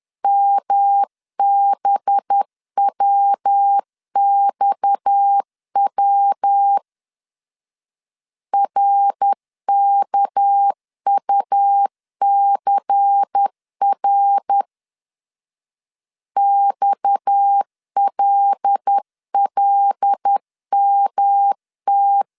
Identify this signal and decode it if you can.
Morse code, heard as a keyed tone, about 10 wpm, 800 Hz. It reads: MBWXW RKUCR XLLMT
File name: CW / DET / 1 6 1500C S A 10wpm 4141